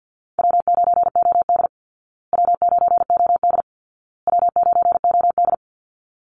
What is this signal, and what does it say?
Morse code, heard as a keyed tone, about 50 wpm, 700 Hz. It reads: W9OD W9OD W9OD
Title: Easy Call Signs Repeated 3x - 50wpm